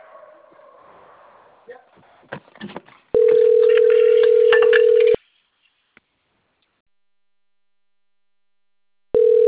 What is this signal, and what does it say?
Ringback tone (North America), 440 + 480 Hz.
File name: rg-1600-unknown- - - .59312